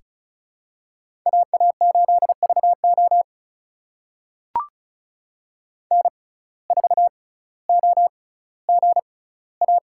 Morse code, heard as a keyed tone, about 36 wpm, spaced as for 18 wpm, 700 Hz. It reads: AA8VO N4OGA
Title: USA Call Signs - 5 Characters - 6x Character Spacing - 35wpm